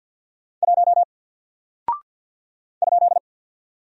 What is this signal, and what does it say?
Morse code, heard as a keyed tone, about 48 wpm, 700 Hz. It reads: 1 ?
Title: Single Character (Letter:Number:Punctuation:Prosign) Alt 50wpm